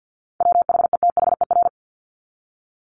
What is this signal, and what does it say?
Morse code, heard as a keyed tone, about 50 wpm, 700 Hz. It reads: WHETHER
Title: Top 501-750 Words 50wpm